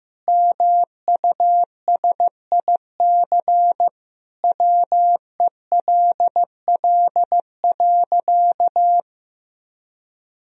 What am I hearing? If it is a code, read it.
Morse code, heard as a keyed tone, about 16 wpm, 700 Hz. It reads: MUSIC WELL.
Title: Sentences from Top 1000 Words - Review Part 1 - 15wpm